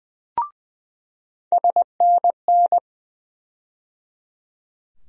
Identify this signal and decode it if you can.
Morse code, heard as a keyed tone, about 20 wpm, 700 Hz. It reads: SNN
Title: USA Call Sign Postfixes - 3 Letters 20wpm